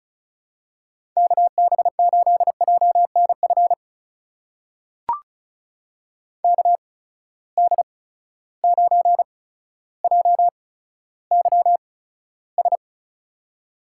Morse code, heard as a keyed tone, about 35 wpm, spaced as for 15 wpm, 700 Hz. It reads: KB8JNF KD9JYS
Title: USA Call Signs - 6 Characters - 8x Character Spacing - 35wpm